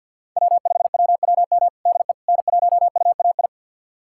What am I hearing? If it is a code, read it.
Morse code, heard as a keyed tone, about 49 wpm, 700 Hz. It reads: W5WWM DE N1UAI
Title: USA Callsign DE Callsign - 5 Characters - 50wpm